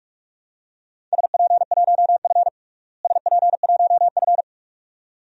Morse code, heard as a keyed tone, about 43 wpm, 700 Hz. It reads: SP1F SP1F
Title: DXCC-100 Call Signs - 4 Characters - Repeated 2x - 45wpm